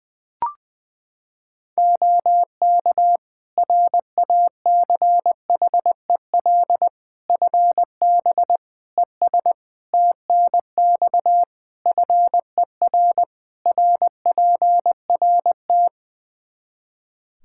Morse code, heard as a keyed tone, about 21 wpm, 700 Hz. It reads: OK RACHEL FB ES TNX FER RPRT
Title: K1USN SST QSO Practice 20wpm